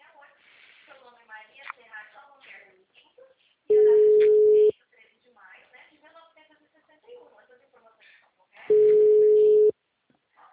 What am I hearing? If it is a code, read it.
Ringback tone (Europe, 425 Hz), with speech.